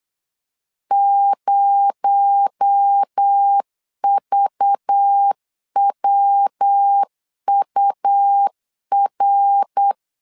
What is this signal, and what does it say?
Morse code, heard as a keyed tone, about 8 wpm, 800 Hz. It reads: 0VWUR
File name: LEZIONI / 6 PKX0 / 6 PKX0 8wpm 3341